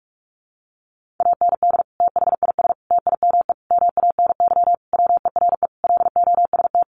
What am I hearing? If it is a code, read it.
Morse code, heard as a keyed tone, about 44 wpm, 700 Hz. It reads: AND THIS TIME MANY WERE LOST
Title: Sentences from Top 1400 Words 45wpm